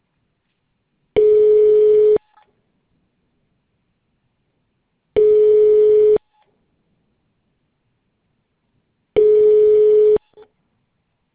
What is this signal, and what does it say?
Ringback tone (425 Hz, 1 s on, 3 s off).